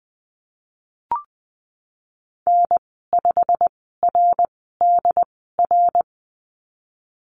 Morse code, heard as a keyed tone, about 20 wpm, spaced as for 15 wpm, 700 Hz. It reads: N5RDR
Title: 088 Lesson 15g - Number 5 - Call Signs 20wpm